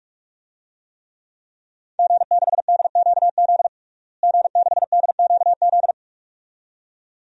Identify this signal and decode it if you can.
Morse code, heard as a keyed tone, about 46 wpm, 700 Hz. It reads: G6DQZ G6DQZ
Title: DXCC-100 Call Signs - 5 Characters - Repeated 2x - 45wpm